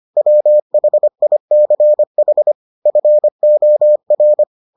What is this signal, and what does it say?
Morse code, heard as a keyed tone, about 25 wpm, 600 Hz. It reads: WHICH FOR